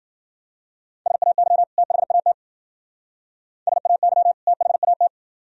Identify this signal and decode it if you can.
Morse code, heard as a keyed tone, about 48 wpm, 700 Hz. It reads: SAY THAT SAY THAT
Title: Speed-Racing - Sets of 2 Words 40wpm